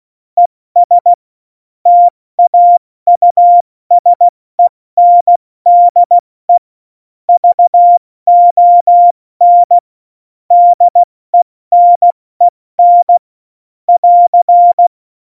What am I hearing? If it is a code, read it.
Morse code, heard as a keyed tone, about 16 wpm, 700 Hz. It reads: ES TAUSENDE VON DENEN <AR>
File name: Morsekurs / Witze / Witz 04. September 2018, 70 F80